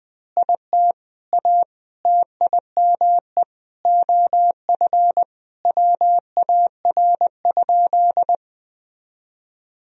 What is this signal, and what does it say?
Morse code, heard as a keyed tone, about 20 wpm, 700 Hz. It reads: IT A TIME OF WAR?
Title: Sentences from Top 1000 Words - Review Part 2 - 20wpm